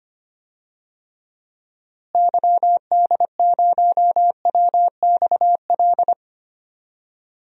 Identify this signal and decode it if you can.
Morse code, heard as a keyed tone, about 25 wpm, 700 Hz. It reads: YD0WXL
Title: DXCC-100 Call Signs - 6 Characters - Repeated 2x - 25wpm